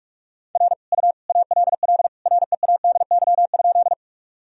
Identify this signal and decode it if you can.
Morse code, heard as a keyed tone, about 46 wpm, 700 Hz. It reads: R U ALL READY?
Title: Sentences from Top 400 Words and Abbreviations 45wpm